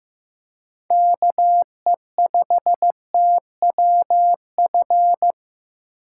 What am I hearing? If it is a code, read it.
Morse code, heard as a keyed tone, about 15 wpm, 700 Hz. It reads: KE5TWF
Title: USA Call Signs - 6 Characters - 6x Character Spacing - 15wpm